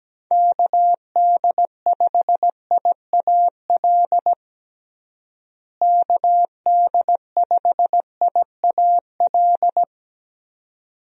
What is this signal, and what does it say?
Morse code, heard as a keyed tone, about 17 wpm, 700 Hz. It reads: KD5IAL KD5IAL
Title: USA Call Signs - 6 Characters - Repeated 2x - 17wpm